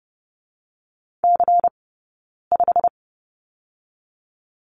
Morse code, heard as a keyed tone, about 32 wpm, spaced as for 9 wpm, 700 Hz. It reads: C5
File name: Lesson 20d - Number 3 - Two Characters 30wpm